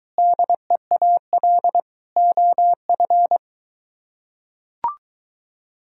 Morse code, heard as a keyed tone, about 23 wpm, 700 Hz. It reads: DEAL OF